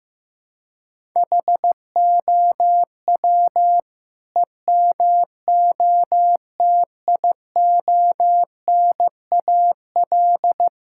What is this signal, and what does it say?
Morse code, heard as a keyed tone, about 15 wpm, 700 Hz. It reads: HOW EMOTIONAL
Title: Sentences from Top 1000 Words - Review Part 5 - 15wpm